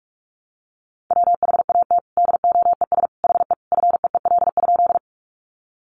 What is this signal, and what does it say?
Morse code, heard as a keyed tone, about 44 wpm, 700 Hz. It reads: WHAT DOES HE FEEL?
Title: Sentences from Top 400 Words 45wpm